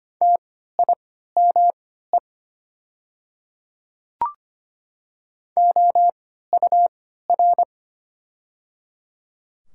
Morse code, heard as a keyed tone, about 25 wpm, spaced as for 14 wpm, 700 Hz. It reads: TIME OUR
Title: 3x Character Spacing - Top 200 Words - 25wpm